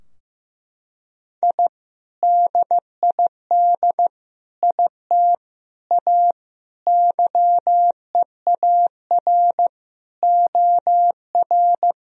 Morse code, heard as a keyed tone, about 15 wpm, 700 Hz. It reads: I DID IT A YEAR OR
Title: Sentences from Top 1000 Words - Review Part 3 - 15wpm